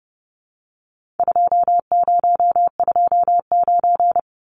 Morse code, heard as a keyed tone, about 30 wpm, 700 Hz. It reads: 2029
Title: Numbers - 4-Digit 30wpm